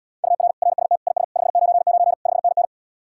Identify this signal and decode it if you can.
Morse code, heard as a keyed tone, about 80 wpm, 700 Hz. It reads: UR RST IS 599 5NN